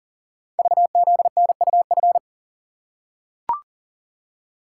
Morse code, heard as a keyed tone, about 38 wpm, 700 Hz. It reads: VZNUF
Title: Code Groups - 5 Letter:Numbers - 40wpm